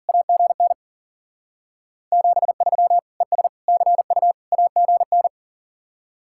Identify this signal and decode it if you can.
Morse code, heard as a keyed tone, about 50 wpm, 700 Hz. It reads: AGN 73 ES CU AGN
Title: Speed-Racing Common QSO Phrases - 40wpm